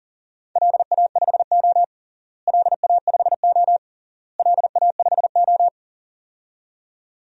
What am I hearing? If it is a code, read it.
Morse code, heard as a keyed tone, about 37 wpm, 700 Hz. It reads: LA5O LA5O LA5O
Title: DXCC-100 Call Signs - 4 Characters - Repeated 3x - 40wpm